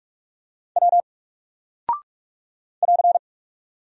Morse code, heard as a keyed tone, about 44 wpm, 700 Hz. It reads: W <AR>
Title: Single Character (Letter:Number:Punctuation:Prosign) Alt 45wpm